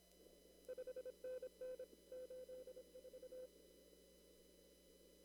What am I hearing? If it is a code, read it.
Morse code, heard as a keyed tone, about 27 wpm, 510 Hz. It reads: 5NN 84
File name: CW / kt5j wpx cw-op-k5pi